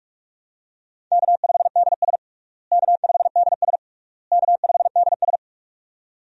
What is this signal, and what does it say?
Morse code, heard as a keyed tone, about 46 wpm, 700 Hz. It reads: K5DS K5DS K5DS
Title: USA Call Signs - 4 Characters - Repeated 3x - 45wpm